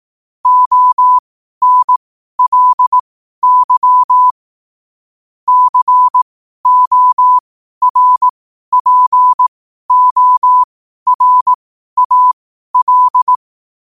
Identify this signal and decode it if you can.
Morse code, heard as a keyed tone, about 18 wpm, spaced as for 12 wpm, 1000 Hz. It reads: ONLY CORPORAL
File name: qotdfiles / 12wpm